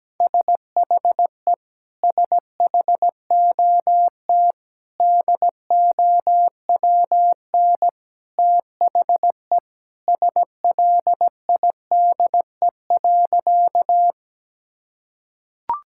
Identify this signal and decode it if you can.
Morse code, heard as a keyed tone, about 17 wpm, 700 Hz. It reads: SHE SHOT DOWN THE SLIDE.
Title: Sentences from Top 1600 Words 17wpm